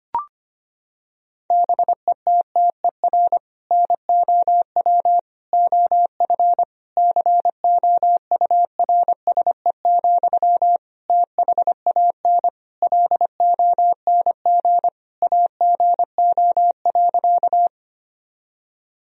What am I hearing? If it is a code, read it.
Morse code, heard as a keyed tone, about 25 wpm, 700 Hz. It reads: BETTER NOW OF COURSE, THAN LONG AGO.
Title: Sentences from Top 1000 Words - Review Part 3 - 25wpm